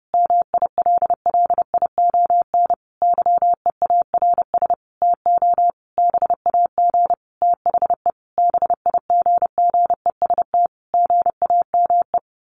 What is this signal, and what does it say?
Morse code, heard as a keyed tone, about 30 wpm, 700 Hz. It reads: MILLION YEARS TO BAG THE BIGGEST GAME
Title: A Sound Of Thunder 30wpm